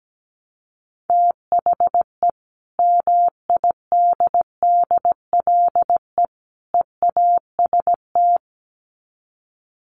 Morse code, heard as a keyed tone, about 17 wpm, 700 Hz. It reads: THE MIDDLE EAST
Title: Sets of 3 Words 17wpm